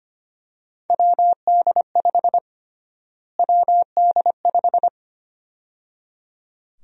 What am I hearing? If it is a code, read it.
Morse code, heard as a keyed tone, about 25 wpm, 700 Hz. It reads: WD5 WD5
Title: USA Call Sign Prefixes - Repeated 2x - 25wpm